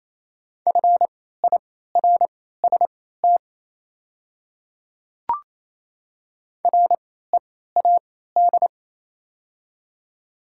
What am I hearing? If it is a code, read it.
Morse code, heard as a keyed tone, about 28 wpm, spaced as for 16 wpm, 690 Hz. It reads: FIRST READ
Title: 3x Character Spacing - Top 500 Words - 28wpm